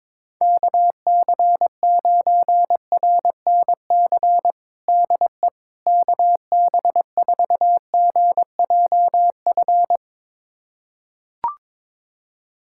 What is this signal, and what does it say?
Morse code, heard as a keyed tone, about 22 wpm, 700 Hz. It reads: KC9RNC DE KB4GJF
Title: USA Callsign DE Callsign 22wpm